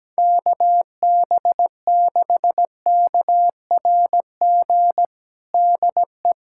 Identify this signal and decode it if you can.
Morse code, heard as a keyed tone, about 17 wpm, 700 Hz. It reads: KB6KRG DE
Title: USA Callsign DE Callsign - 6 Characters - 17wpm